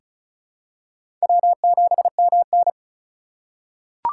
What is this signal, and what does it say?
Morse code, heard as a keyed tone, about 36 wpm, 700 Hz. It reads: W7MN
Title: USA Call Signs - 3x Character Spacing - 35wpm